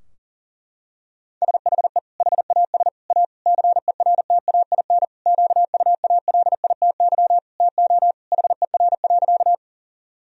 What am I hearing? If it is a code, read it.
Morse code, heard as a keyed tone, about 37 wpm, 700 Hz. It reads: SHE HAS A CERTAIN QUALITY TO HER.
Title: Sentences from Top 1000 Words - Review Part 4 - 40wpm